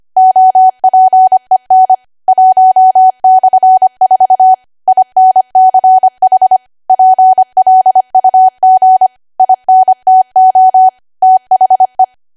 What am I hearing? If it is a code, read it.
Morse code, heard as a keyed tone, about 25 wpm, 750 Hz. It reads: OPEN 1/4 INCH PLUG INTO THE
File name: Archive / 25 WPM / 25WPM